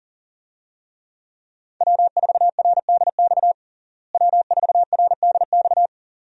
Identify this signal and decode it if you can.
Morse code, heard as a keyed tone, about 37 wpm, 700 Hz. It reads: W4RDX W4RDX
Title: USA Call Signs - 5 Characters - Repeated 2x - 40wpm